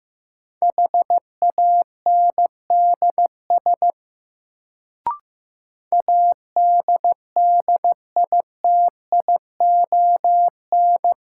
Morse code, heard as a keyed tone, about 15 wpm, 700 Hz. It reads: HANDS ADDITION
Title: Sentences from Top 1000 Words - Review Part 6 - New Word Drill - 15wpm